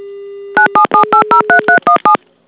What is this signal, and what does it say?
DTMF, heard as telephone touch tones, tones of about 100 ms. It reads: 977803317
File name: ivr / Debug / DTMF7